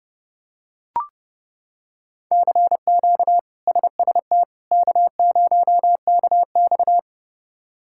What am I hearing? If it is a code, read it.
Morse code, heard as a keyed tone, about 30 wpm, 700 Hz. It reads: CQ SST K0KX
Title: K1USN SST QSO Practice - 30wpm